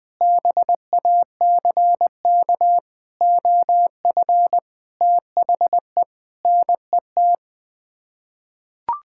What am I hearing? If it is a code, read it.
Morse code, heard as a keyed tone, about 20 wpm, 700 Hz. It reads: BACK OF THE NET
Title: Sets of 9 Words - 20wpm